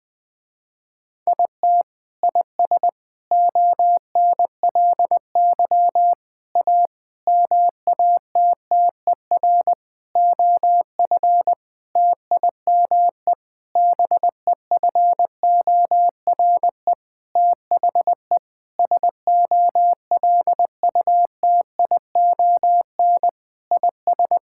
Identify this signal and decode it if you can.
Morse code, heard as a keyed tone, about 20 wpm, 700 Hz. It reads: IT IS ONLY A MATTER OF TIME BEFORE THE SOLUTION IS
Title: Sentences with 9-Word Phrases - Spoken Beginning - 20wpm